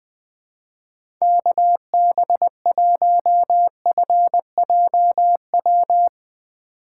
Morse code, heard as a keyed tone, about 20 wpm, 700 Hz. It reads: KB1FJW
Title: USA Call Signs - 6 Characters - 3x Character Spacing - 20wpm